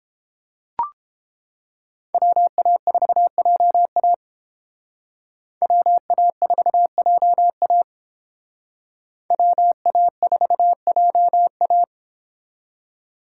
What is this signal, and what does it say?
Morse code, heard as a keyed tone, about 31 wpm, 700 Hz. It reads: WA4JA WA4JA WA4JA